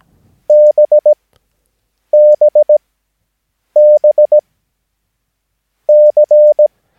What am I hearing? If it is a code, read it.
Morse code, heard as a keyed tone, about 17 wpm, spaced as for 6 wpm, 600 Hz. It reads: BBBC
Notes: Övningsmaterial med alfabetet A till Ö samt siffror 0 till 9 enligt International/Continental Morse Code.